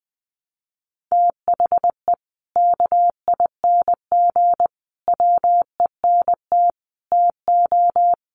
Morse code, heard as a keyed tone, about 20 wpm, 700 Hz. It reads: THE KING WENT TO
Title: Sentences from Top 400 Words 20wpm